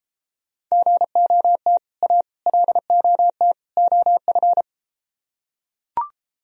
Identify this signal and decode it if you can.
Morse code, heard as a keyed tone, about 32 wpm, 700 Hz. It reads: GOT A LOT OF